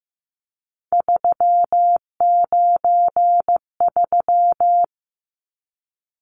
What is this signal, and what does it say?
Morse code, heard as a keyed tone, about 15 wpm, 700 Hz. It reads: 393
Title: Numbers - 3-Digit 15wpm